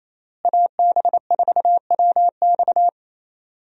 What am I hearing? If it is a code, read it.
Morse code, heard as a keyed tone, about 27 wpm, 700 Hz. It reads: AB4WX
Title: USA Call Signs - 5 Characters - Repeated 3x - 28wpm